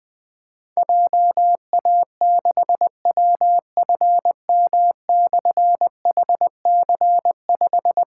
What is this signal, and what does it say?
Morse code, heard as a keyed tone, about 20 wpm, 700 Hz. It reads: JA6WFM/HC5
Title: Difficult Call Signs 20wpm